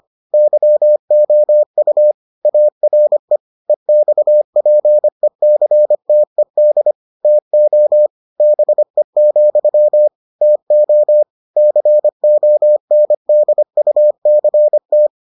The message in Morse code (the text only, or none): YOU ARE EXPECTED TO BE, TO CONDUCT